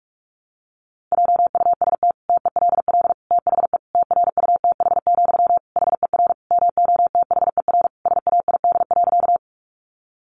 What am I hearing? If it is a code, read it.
Morse code, heard as a keyed tone, about 44 wpm, 700 Hz. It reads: JUST TELL THE TRUTH, HER MOTHER SAID.
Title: Sentences from Top 800 Words 45wpm